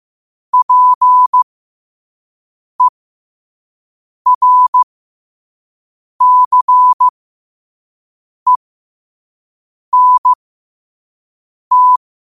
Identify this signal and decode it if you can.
Morse code, heard as a keyed tone, about 15 wpm, spaced as for 5 wpm, 1000 Hz. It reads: PERCENT
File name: qotdfiles / 5wpm